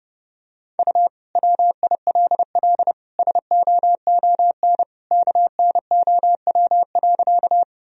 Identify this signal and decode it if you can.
Morse code, heard as a keyed tone, about 30 wpm, 700 Hz. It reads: U WILL SOON KNOW.
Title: Sentences from Top 400 Words and Abbreviations 30wpm